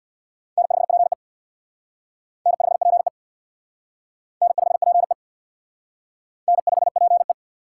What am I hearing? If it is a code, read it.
Morse code, heard as a keyed tone, about 60 wpm, 700 Hz. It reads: N5PE N5PE N5PE N5PE
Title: Speed Racing - USA Call Signs 50wpm